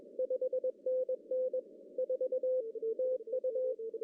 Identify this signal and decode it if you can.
Morse code, heard as a keyed tone, about 21 wpm, 510 Hz. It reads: 5NN 4 TU
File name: CW / cqww-cw-k5tr